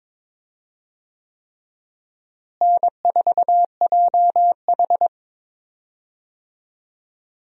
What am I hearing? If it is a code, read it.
Morse code, heard as a keyed tone, about 22 wpm, 700 Hz. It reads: N4JH
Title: USA Call Signs - 4 Characters - Repeated 3x - 22wpm